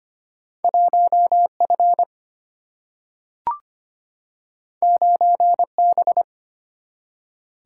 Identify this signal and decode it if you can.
Morse code, heard as a keyed tone, about 25 wpm, 700 Hz. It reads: 1F 9B
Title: Single-Letter Single-Number 25wpm